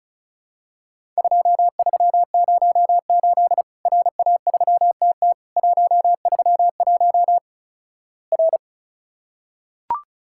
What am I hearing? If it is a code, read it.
Morse code, heard as a keyed tone, about 36 wpm, 700 Hz. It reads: 2308 RA3TT 131 R
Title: Worked All Europe DX Contest - QTC Exchange - 35wpm